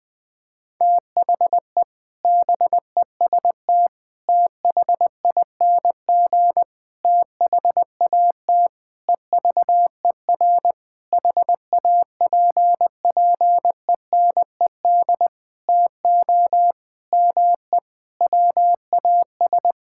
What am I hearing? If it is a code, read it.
Morse code, heard as a keyed tone, about 20 wpm, 700 Hz. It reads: THE BEST THING THAT EVER HAPPENED TO ME WAS
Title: Sentences with 7-Word Phrases - Spoken Beginning - 20wpm